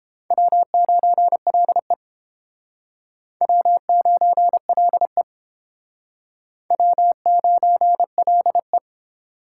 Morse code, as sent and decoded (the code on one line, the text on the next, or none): .-- ----. .-.. / .-- ----. .-.. / .-- ----. .-.. .
W9LE W9LE W9LE